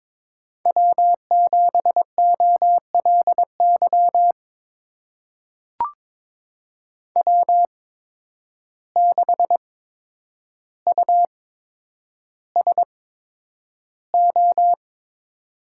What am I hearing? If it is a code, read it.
Morse code, heard as a keyed tone, about 22 wpm, spaced as for 9 wpm, 700 Hz. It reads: W7OLY W6USO
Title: USA Call Signs - 5 Characters - 8x Character Spacing - 22wpm